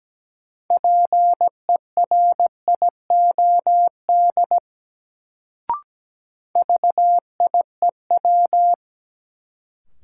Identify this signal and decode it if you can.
Morse code, heard as a keyed tone, about 17 wpm, 700 Hz. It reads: PERIOD VIEW
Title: Sets of 5 Words - New Word Drill - 17wpm